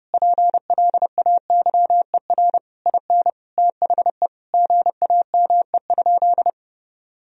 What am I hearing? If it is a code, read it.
Morse code, heard as a keyed tone, about 30 wpm, 700 Hz. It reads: PLAYER IN THE GAME?
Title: Sentences from Top 1000 Words - Review Part 4 - 30wpm